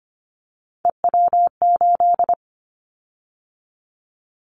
Morse code, heard as a keyed tone, about 25 wpm, 700 Hz. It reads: EW8
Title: Callsign Prefixes for Top 100 DXCC - 25wpm